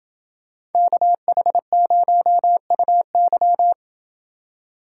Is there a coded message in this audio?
Yes: Morse code from 0.8 to 3.7 s, heard as a keyed tone, about 29 wpm, 700 Hz. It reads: KH0UY